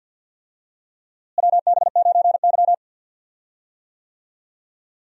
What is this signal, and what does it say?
Morse code, heard as a keyed tone, about 53 wpm, 700 Hz. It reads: WB9Y
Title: USA Call Signs - 4 Characters - Repeated 3x - 50wpm